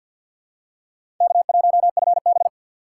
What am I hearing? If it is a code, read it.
Morse code, heard as a keyed tone, about 50 wpm, 700 Hz. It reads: K1FB
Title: USA Call Signs - 4 Characters - 2x Character Spacing - 50wpm